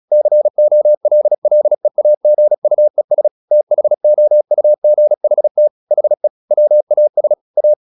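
Morse code, heard as a keyed tone, about 36 wpm, 600 Hz. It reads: COLLEAGUES THOUGHT HE WAS A
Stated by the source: Morse practice at 36 wpm, text of Ken Follett's novel Never, from CW med Gnister.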